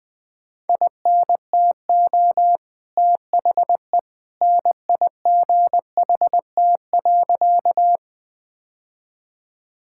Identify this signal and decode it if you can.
Morse code, heard as a keyed tone, about 20 wpm, 700 Hz. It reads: INTO THE NIGHT.
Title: Sentences with 2-Word Phrases - 20wpm